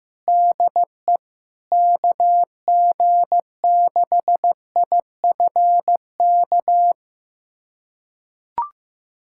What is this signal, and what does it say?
Morse code, heard as a keyed tone, about 15 wpm, 700 Hz. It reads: DE KG6IFK
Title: USA Callsign DE Callsign 15wpm